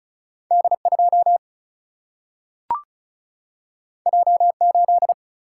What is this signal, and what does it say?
Morse code, heard as a keyed tone, about 36 wpm, 700 Hz. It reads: D2 J8
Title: Single-Letter Single-Number 35wpm